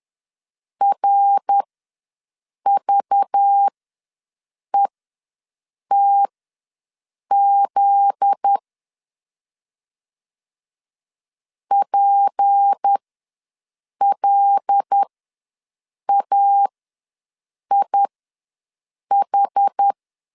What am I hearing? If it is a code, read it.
Morse code, heard as a keyed tone, about 10 wpm, spaced as for 6 wpm, 800 Hz. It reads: RVETZ PLAIH